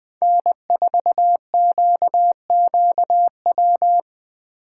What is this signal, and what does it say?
Morse code, heard as a keyed tone, about 20 wpm, 700 Hz. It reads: N4QQW
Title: USA Call Signs - 5 Characters - 4x Character Spacing - 20wpm